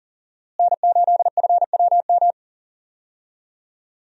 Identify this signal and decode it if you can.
Morse code, heard as a keyed tone, about 40 wpm, 700 Hz. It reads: N8FWM
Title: USA Call Signs - 5 Characters - Repeated 2x - 40wpm